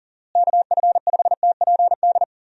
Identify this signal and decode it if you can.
Morse code, heard as a keyed tone, about 40 wpm, 700 Hz. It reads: KF5TPD
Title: USA Call Signs - 6 Characters - Repeated 2x - 40wpm